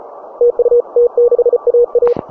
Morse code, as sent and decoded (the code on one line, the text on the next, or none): - ..- / - -.... .- .-
TU T6AA